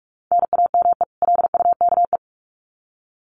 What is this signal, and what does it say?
Morse code, heard as a keyed tone, about 44 wpm, 700 Hz. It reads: NAME LUKE
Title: Common QSO Phrases 45wpm